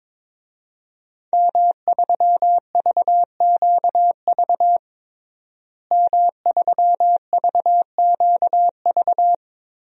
Morse code, heard as a keyed tone, about 22 wpm, 700 Hz. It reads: M3VQV M3VQV
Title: DXCC-100 Call Signs - 5 Characters - Repeated 3x - 22wpm